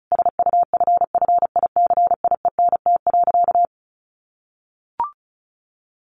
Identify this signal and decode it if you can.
Morse code, heard as a keyed tone, about 34 wpm, 700 Hz. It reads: SUFFICIENT.
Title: Sentences from Top 1400 Words 35wpm